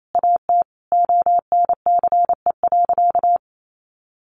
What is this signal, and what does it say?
Morse code, heard as a keyed tone, about 28 wpm, 700 Hz. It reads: AT ONCE.
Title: Sentences from Top 1100 Words 28wpm